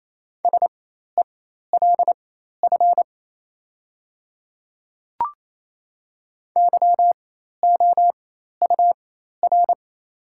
Morse code, heard as a keyed tone, about 27 wpm, spaced as for 13 wpm, 690 Hz. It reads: SELF YOUR
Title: 4x Character Spacing - Top 200 Words - 28wpm